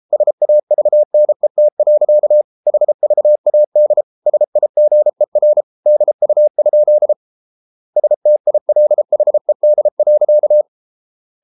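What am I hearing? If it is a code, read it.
Morse code, heard as a keyed tone, about 33 wpm, 600 Hz. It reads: SAVNET. HVAD SIGER DU? STILHED.
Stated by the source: Morse code at 33 wpm, QRQ.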